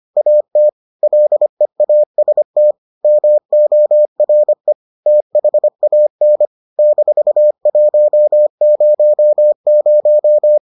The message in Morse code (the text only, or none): AT LEAST MORE THAN -100